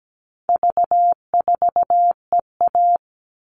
Morse code, heard as a keyed tone, about 17 wpm, 700 Hz. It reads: V4EA
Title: All Cut Numbers - 4-Digit 17wpm